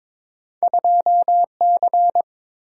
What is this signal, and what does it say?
Morse code, heard as a keyed tone, about 22 wpm, 700 Hz. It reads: 2C
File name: practice / 22wpm / Two Letter-Number Combinations - 22wpm